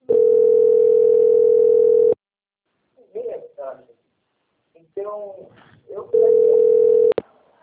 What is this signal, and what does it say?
Ringback tone (North America), 440 + 480 Hz, with speech.
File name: out- - - - - .16470